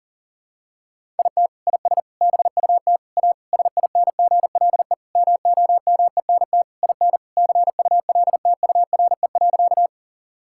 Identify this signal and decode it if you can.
Morse code, heard as a keyed tone, about 40 wpm, 700 Hz. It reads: IT IS BUT A SINGLE MOMENT IN CULTURE.
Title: Sentences from Top 1000 Words - Review Part 6 - 40wpm